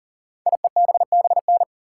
Morse code, heard as a keyed tone, about 39 wpm, 700 Hz. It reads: IEBBN
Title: Code Groups - 5 Letters - 40wpm